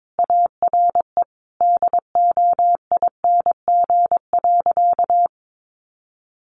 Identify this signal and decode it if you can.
Morse code, heard as a keyed tone, about 22 wpm, 700 Hz. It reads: ARE DOING.
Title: Sentences from Top 1400 Words 22wpm